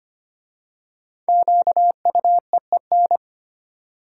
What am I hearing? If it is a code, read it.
Morse code, heard as a keyed tone, about 25 wpm, 700 Hz. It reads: QUEEN
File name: 5 Letter Words - First 3 Letters Given - 25wpm